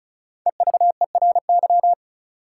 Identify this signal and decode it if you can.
Morse code, heard as a keyed tone, about 36 wpm, 700 Hz. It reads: EVERY
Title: Sets of 5 Words - New Word Drill - 35wpm